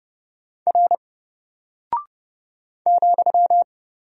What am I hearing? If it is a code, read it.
Morse code, heard as a keyed tone, about 30 wpm, 700 Hz. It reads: R ,
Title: Single Character (Letter:Number:Punctuation:Prosign) 30wpm